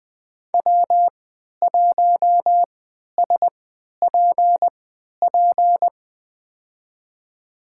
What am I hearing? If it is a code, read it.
Morse code, heard as a keyed tone, about 20 wpm, spaced as for 11 wpm, 700 Hz. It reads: W1SPP
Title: USA Call Signs - 5 Characters - 3x Character Spacing - 20wpm